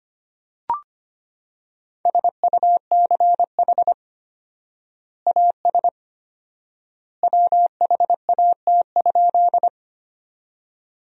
Morse code, heard as a keyed tone, about 25 wpm, 700 Hz. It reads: SUCH AS WHAT?
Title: 4x Word-Spacing - Sentences from Top 200 Words - 25wpm